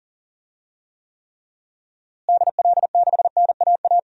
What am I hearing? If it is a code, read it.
Morse code, heard as a keyed tone, about 40 wpm, 700 Hz. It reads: DL6NAA
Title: DXCC-100 Call Signs - 6 Characters - Repeated 3x - 40wpm